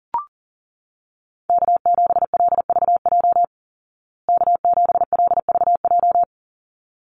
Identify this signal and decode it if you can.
Morse code, heard as a keyed tone, about 40 wpm, 700 Hz. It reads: K7LVJ K7LVJ
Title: Call Signs Repeated 3x - 40wpm